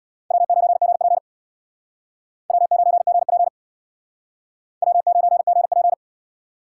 Morse code, heard as a keyed tone, about 62 wpm, 700 Hz. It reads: W9GP W9GP W9GP
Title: Speed Racing - USA 1x2 Call Signs 50wpm